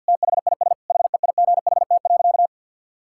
Morse code, heard as a keyed tone, about 51 wpm, 700 Hz. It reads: THIS HEIGHT.
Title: Sentences from Top 1600 Words 50wpm